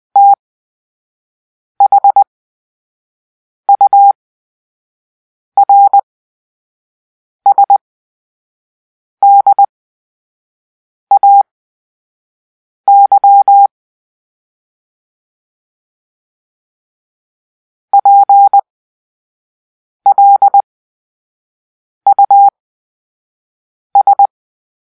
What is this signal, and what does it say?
Morse code, heard as a keyed tone, about 20 wpm, spaced as for 5 wpm, 800 Hz. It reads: THURSDAY PLUS
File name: Grp-3 Drill-4